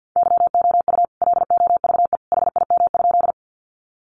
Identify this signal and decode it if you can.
Morse code, heard as a keyed tone, about 50 wpm, 700 Hz. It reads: YOU LOVE HIM?
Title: Sentences from Top 400 Words 50wpm